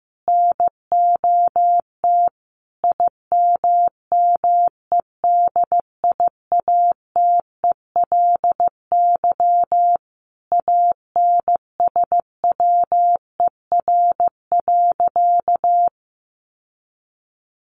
Morse code, heard as a keyed tone, about 15 wpm, 700 Hz. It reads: NOT IMMEDIATELY ANSWER.
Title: Sentences from Top 1100 Words 15wpm